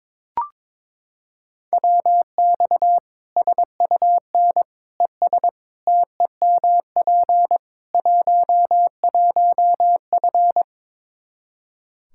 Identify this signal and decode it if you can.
Morse code, heard as a keyed tone, about 21 wpm, 700 Hz. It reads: WX SUN ES TEMP 11F
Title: Practice QSO - Exchange 2 - Info - 22wpm